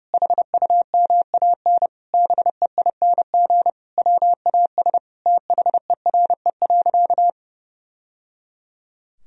Morse code, heard as a keyed tone, about 32 wpm, 700 Hz. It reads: HUMAN BEING WAS THERE.
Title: Sentences from Top 1000 Words - Review Part 4 - 30wpm